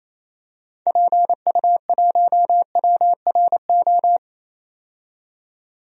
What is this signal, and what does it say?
Morse code, heard as a keyed tone, about 28 wpm, 700 Hz. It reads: PU1WRO
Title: DXCC-100 Call Signs - 6 Characters - 28wpm